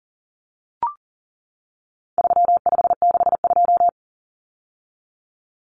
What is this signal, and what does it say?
Morse code, heard as a keyed tone, about 40 wpm, 700 Hz. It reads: 3562
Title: Numbers - 4-Digit 40wpm